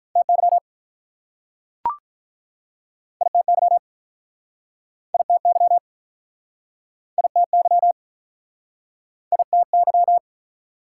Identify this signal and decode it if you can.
Morse code, heard as a keyed tone, about 48 wpm, 700 Hz. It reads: TY ITY ITY ITY ITY